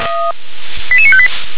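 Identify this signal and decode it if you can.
Five-tone selective call (ZVEI), heard as quick runs of five tones, tones of about 70 ms. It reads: 88958
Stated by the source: ZVEI five-tone call received by SDR.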